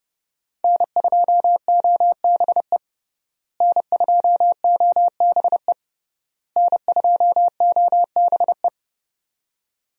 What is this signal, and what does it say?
Morse code, heard as a keyed tone, about 31 wpm, 700 Hz. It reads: N2OBE N2OBE N2OBE
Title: USA Call Signs - 5 Characters - Repeated 3x - 30wpm